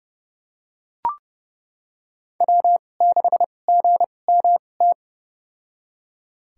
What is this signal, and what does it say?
Morse code, heard as a keyed tone, about 30 wpm, spaced as for 22 wpm, 700 Hz. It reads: W6GMT
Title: USA Call Signs - 2x Character Spacing - 30wpm